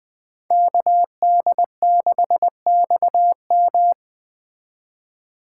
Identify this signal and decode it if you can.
Morse code, heard as a keyed tone, about 20 wpm, 700 Hz. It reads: KD6XM
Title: Speed Racing - USA 2x2 Call Signs 17wpm